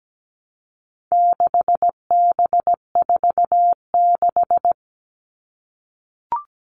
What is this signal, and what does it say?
Morse code, heard as a keyed tone, about 17 wpm, 700 Hz. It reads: 6B46
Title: All Cut Numbers - 4-Digit 17wpm